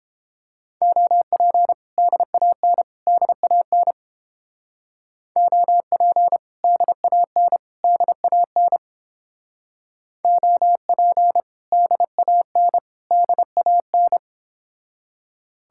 Morse code, heard as a keyed tone, about 31 wpm, 700 Hz. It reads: OP DAN DAN OP DAN DAN OP DAN DAN